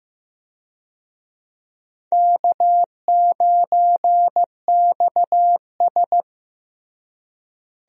Morse code, heard as a keyed tone, about 15 wpm, 700 Hz. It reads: K9XS
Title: USA Call Signs - 4 Characters - Repeated 3x - 15wpm